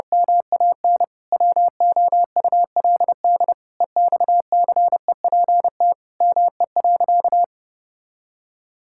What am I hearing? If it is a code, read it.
Morse code, heard as a keyed tone, about 30 wpm, 700 Hz. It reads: MAN WOULD EXCEPT ME.
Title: Sentences from Top 1600 Words 30wpm